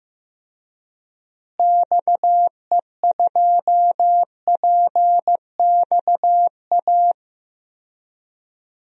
Morse code, heard as a keyed tone, about 15 wpm, 700 Hz. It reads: XE2PXA
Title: DXCC-100 Call Signs - 6 Characters - Repeated 3x - 15wpm